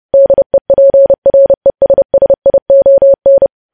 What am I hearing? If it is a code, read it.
Morse code, heard as a keyed tone, about 30 wpm, 550 Hz. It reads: DEPRESSION
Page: news headlines in morse code at 30 wpm